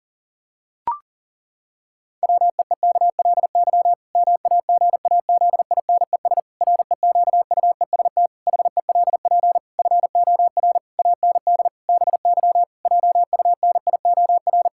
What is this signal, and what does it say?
Morse code, heard as a keyed tone, about 40 wpm, 700 Hz. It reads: WEEKLY MAGAZINES REQUEST HELP FOR AND BY JUNIOR
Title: Pangram Challenge v2 - 40wpm